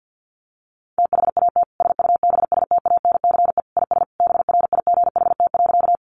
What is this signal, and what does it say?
Morse code, heard as a keyed tone, about 50 wpm, 700 Hz. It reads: THAT SUBSTANCE IS BRIGHT.
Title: Sentences from Top 1500 Words 50wpm